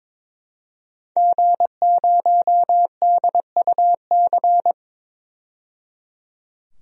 Morse code, heard as a keyed tone, about 22 wpm, 700 Hz. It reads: G0DUC
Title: DXCC-100 Call Signs - 5 Characters - 22wpm